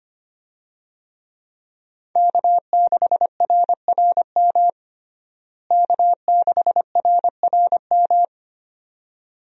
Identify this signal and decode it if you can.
Morse code, heard as a keyed tone, about 25 wpm, 700 Hz. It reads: K6RRM K6RRM
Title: USA Call Signs - 5 Characters - Repeated 2x - 25wpm